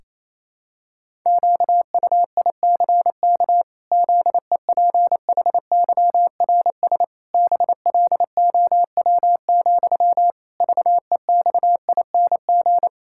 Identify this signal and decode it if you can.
Morse code, heard as a keyed tone, about 28 wpm, 700 Hz. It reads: QUICK ZEPHYRS BLOW, VEXING
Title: Pangram Challenge v2 - 28wpm